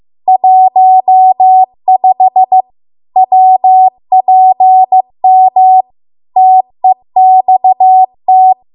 Morse code, heard as a keyed tone, about 15 wpm, 750 Hz. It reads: 15 WPM TEXT